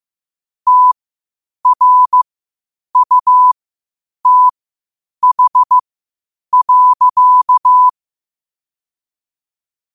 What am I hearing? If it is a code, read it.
Morse code, heard as a keyed tone, about 15 wpm, spaced as for 8 wpm, 1000 Hz. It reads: TRUTH.
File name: qotdfiles / 8wpm